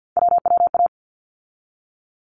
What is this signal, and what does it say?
Morse code, heard as a keyed tone, about 48 wpm, 700 Hz. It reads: WWA
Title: ARRL Sections 50wpm